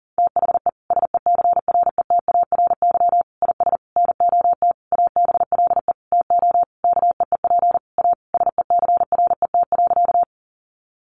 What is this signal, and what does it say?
Morse code, heard as a keyed tone, about 40 wpm, 700 Hz. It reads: THE SECRETARY IS NOT ABLE TO KEEP A SECRET.
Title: Sentences from Top 1300 Words 40wpm